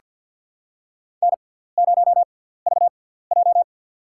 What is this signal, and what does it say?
Morse code, heard as a keyed tone, about 48 wpm, spaced as for 17 wpm, 700 Hz. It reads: N0VJ
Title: USA Call Signs - 4 Characters - 6x Character Spacing - 50wpm